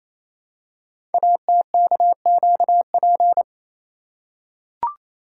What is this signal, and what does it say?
Morse code, heard as a keyed tone, about 29 wpm, 700 Hz. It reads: ATKQP
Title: Code Groups - 5 Letters - 28wpm